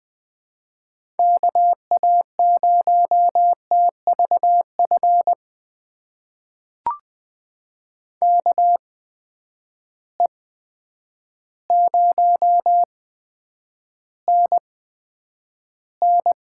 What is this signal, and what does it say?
Morse code, heard as a keyed tone, about 20 wpm, spaced as for 9 wpm, 700 Hz. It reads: KA0TVF KE0NN
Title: USA Call Signs - 6 Characters - 8x Character Spacing - 20wpm